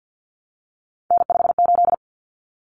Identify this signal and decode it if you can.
Morse code, heard as a keyed tone, about 50 wpm, 700 Hz. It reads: N58
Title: Common Cut Numbers - 3-Digit 50wpm